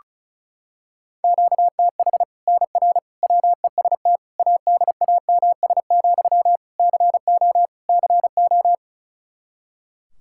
Morse code, heard as a keyed tone, about 36 wpm, 700 Hz. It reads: QTH NR WEST ADAMS, CO CO
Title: Practice QSO - Exchange 1 - Beginning - 35wpm